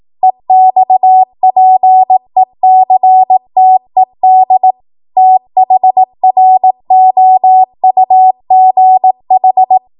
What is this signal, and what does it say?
Morse code, heard as a keyed tone, about 18 wpm, 750 Hz. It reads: EXPECTED THROUGH
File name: Morse / 18WPM